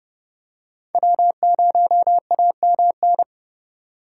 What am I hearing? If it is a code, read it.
Morse code, heard as a keyed tone, about 30 wpm, 700 Hz. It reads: W0AMN
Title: USA Call Signs - 5 Characters - 2x Character Spacing - 30wpm